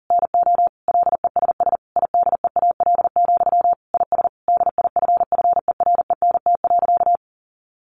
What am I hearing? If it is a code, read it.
Morse code, heard as a keyed tone, about 40 wpm, 700 Hz. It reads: NO LESS IDEAL, IS DIFFERENT.
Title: Sentences from Top 1400 Words 40wpm